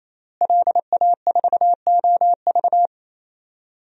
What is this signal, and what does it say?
Morse code, heard as a keyed tone, about 27 wpm, 700 Hz. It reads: LA4OV
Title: DXCC-100 Call Signs - 5 Characters - Repeated 3x - 28wpm